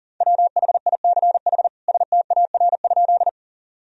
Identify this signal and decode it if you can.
Morse code, heard as a keyed tone, about 40 wpm, 700 Hz. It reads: WHICH STAR?
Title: Sentences from Top 1000 Words - Review Part 2 - 40wpm